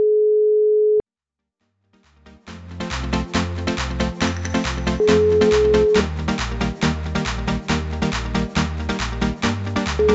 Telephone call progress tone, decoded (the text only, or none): ringback tone